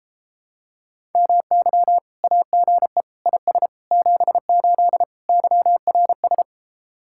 Morse code, heard as a keyed tone, about 29 wpm, 700 Hz. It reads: MY AGE IS 78 YRS